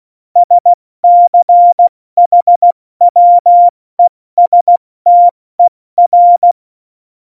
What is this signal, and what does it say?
Morse code, heard as a keyed tone, about 16 wpm, 700 Hz. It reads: SCHWESTER
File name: Morsekurs / Witze / Witz 15. September 2020, 70 F80